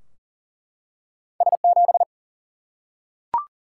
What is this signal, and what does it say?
Morse code, heard as a keyed tone, about 40 wpm, 700 Hz. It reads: S 7
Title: Single-Letter Single-Number 40wpm